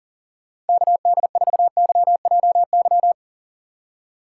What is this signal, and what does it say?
Morse code, heard as a keyed tone, about 52 wpm, 700 Hz. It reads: KD4YJY